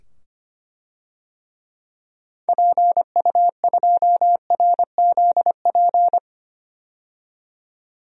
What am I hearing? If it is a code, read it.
Morse code, heard as a keyed tone, about 25 wpm, 700 Hz. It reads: PU2RZP
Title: DXCC-100 Call Signs - 6 Characters - 25wpm